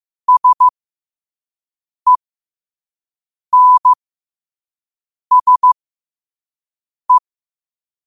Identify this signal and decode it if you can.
Morse code, heard as a keyed tone, about 15 wpm, spaced as for 5 wpm, 1000 Hz. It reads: SENSE
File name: qotdfiles / 5wpm